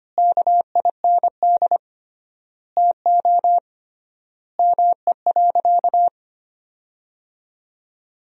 Morse code, heard as a keyed tone, about 25 wpm, 700 Hz. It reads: KIND TO ME.
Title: 3x Word-Spacing - Sentences from Top 200 Words - 25wpm